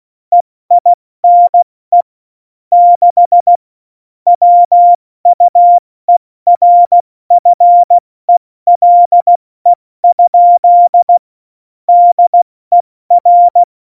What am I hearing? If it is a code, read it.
Morse code, heard as a keyed tone, about 16 wpm, 700 Hz. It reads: EINE 6 WUERFELE? DER
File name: Morsekurs / Witze / Witz 22. Dezember 2020, 70 F80